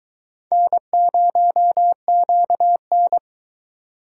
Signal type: Morse code

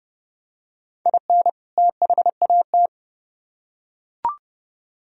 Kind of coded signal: Morse code

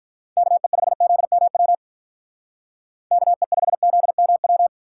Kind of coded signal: Morse code

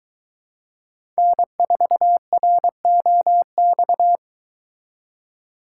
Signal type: Morse code

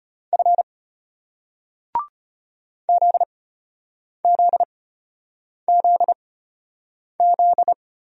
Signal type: Morse code

